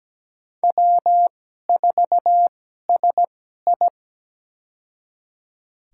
Morse code, heard as a keyed tone, about 17 wpm, spaced as for 12 wpm, 700 Hz. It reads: W4SI
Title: USA Call Signs - 4 Characters - 2x Character Spacing - 17wpm